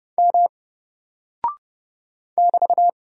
Morse code, heard as a keyed tone, about 30 wpm, 700 Hz. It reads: M <BT>
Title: Single Character (Letter:Number:Punctuation:Prosign) Alt 30wpm